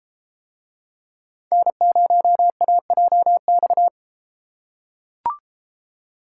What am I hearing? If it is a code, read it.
Morse code, heard as a keyed tone, about 32 wpm, 700 Hz. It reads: N0AJX